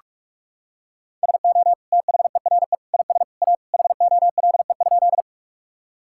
Morse code, heard as a keyed tone, about 45 wpm, 700 Hz. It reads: SO THERE IS A HOLE?
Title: Sentences from Top 1600 Words 45wpm